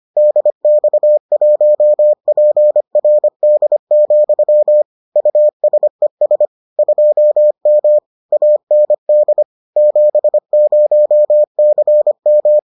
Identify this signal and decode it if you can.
Morse code, heard as a keyed tone, about 25 wpm, 600 Hz. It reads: DX1PRD, USES 2M AND 70CM